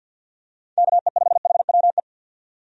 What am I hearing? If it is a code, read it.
Morse code, heard as a keyed tone, about 52 wpm, 700 Hz. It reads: KE5HWE
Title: USA Call Signs - 6 Characters - 50wpm